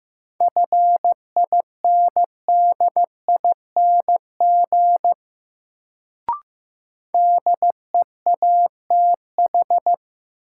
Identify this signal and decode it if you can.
Morse code, heard as a keyed tone, about 15 wpm, 700 Hz. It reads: FINDING DEATH
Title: Sentences from Top 1100 Words - New Word Drill Alt - 15wpm